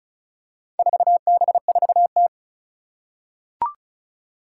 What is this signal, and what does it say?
Morse code, heard as a keyed tone, about 36 wpm, 690 Hz. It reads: 4B4T
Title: Four Challenging Letter-Number Combinations 35wpm